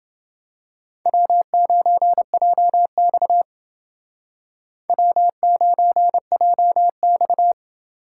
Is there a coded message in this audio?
Yes: Morse code from 1.1 to 7.5 s, heard as a keyed tone, about 28 wpm, 700 Hz. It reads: W9JX W9JX